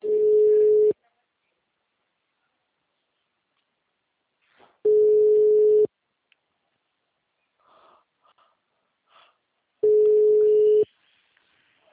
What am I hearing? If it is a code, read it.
Ringback tone (Europe, 425 Hz), with speech.